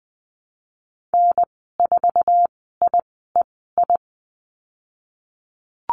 Morse code, heard as a keyed tone, about 20 wpm, spaced as for 15 wpm, 700 Hz. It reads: N4IEI
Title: 088 Lesson 15g - Number 5 - Call Signs 20wpm